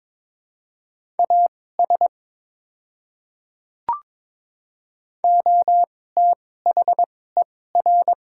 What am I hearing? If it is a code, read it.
Morse code, heard as a keyed tone, about 22 wpm, spaced as for 16 wpm, 700 Hz. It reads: AS OTHER
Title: 2x Character Spacing - Top 100 Words - 22wpm